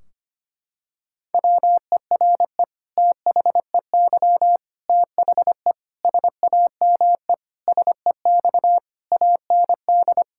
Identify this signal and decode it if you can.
Morse code, heard as a keyed tone, about 25 wpm, 700 Hz. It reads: WERE THEY THE SAME SEX AND
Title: Sentences from Top 1600 Words 25wpm